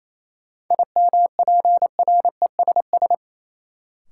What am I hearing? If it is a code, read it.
Morse code, heard as a keyed tone, about 28 wpm, 700 Hz. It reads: IMPRESS
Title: Sentences from Top 1600 Words - New Word Drill - 28wpm